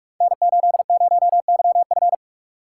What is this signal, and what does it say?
Morse code, heard as a keyed tone, about 44 wpm, 700 Hz. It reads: N80YF
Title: Code Groups - 5 Letter:Numbers - 45wpm